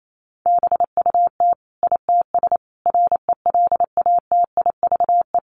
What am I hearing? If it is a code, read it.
Morse code, heard as a keyed tone, about 28 wpm, 700 Hz. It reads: BUT ITS RELATIVE
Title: Sentences from Top 1400 Words 28wpm